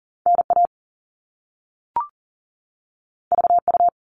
Morse code, heard as a keyed tone, about 42 wpm, 700 Hz. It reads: NA VU
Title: All Cut Numbers - 2-Digit 40wpm